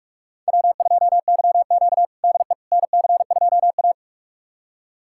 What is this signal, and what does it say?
Morse code, heard as a keyed tone, about 44 wpm, 700 Hz. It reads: W2YQ DE NC2A
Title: USA Callsign DE Callsign - 4 Characters - 45wpm